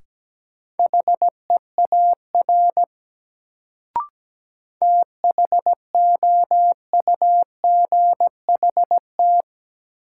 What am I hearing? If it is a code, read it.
Morse code, heard as a keyed tone, about 17 wpm, 700 Hz. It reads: HEAR THOUGHT
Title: Sentences from Top 300 Words - New Word Drill - 17wpm